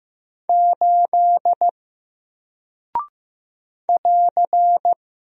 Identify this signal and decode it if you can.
Morse code, heard as a keyed tone, about 15 wpm, 700 Hz. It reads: 8 <AR>
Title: Single Character (Letter:Number:Punctuation:Prosign) Alt 15wpm